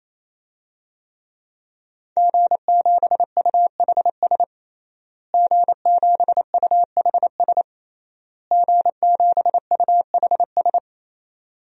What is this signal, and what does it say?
Morse code, heard as a keyed tone, about 27 wpm, 700 Hz. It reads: G7UHS G7UHS G7UHS
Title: DXCC-100 Call Signs - 5 Characters - Repeated 3x - 28wpm